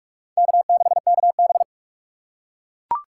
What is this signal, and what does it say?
Morse code, heard as a keyed tone, about 44 wpm, 700 Hz. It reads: K6KB
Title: USA Call Signs - 4 Characters - 3x Character Spacing - 45wpm